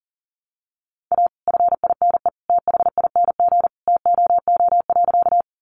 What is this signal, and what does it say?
Morse code, heard as a keyed tone, about 40 wpm, 700 Hz. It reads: A FINE THING TOO.
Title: Sentences from Top 500 Words 40wpm